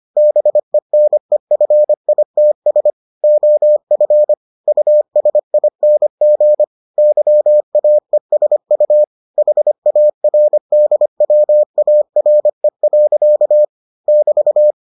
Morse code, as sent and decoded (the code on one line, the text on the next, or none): -... -. ..-. .. - ... / --- ..-. / ..- ... .. -. --. / -.-- .- . ... ..- / .... .- .-. -.. .-- .- .-. .-.-.- / -...-
BENEFITS OF USING YAESU HARDWARE. <BT>